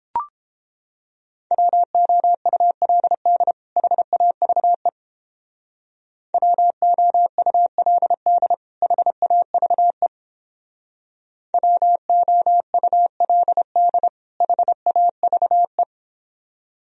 Morse code, heard as a keyed tone, about 29 wpm, 700 Hz. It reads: WOULD HAVE WOULD HAVE WOULD HAVE